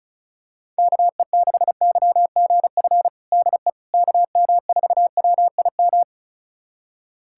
Morse code, heard as a keyed tone, about 36 wpm, 700 Hz. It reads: KE6YGF DE KM4WIM
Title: USA Callsign DE Callsign - 6 Characters - 35wpm